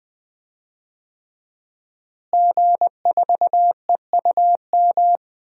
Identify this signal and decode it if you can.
Morse code, heard as a keyed tone, about 21 wpm, 700 Hz. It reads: G4EUM
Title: DXCC-100 Call Signs - 5 Characters - Repeated 3x - 20wpm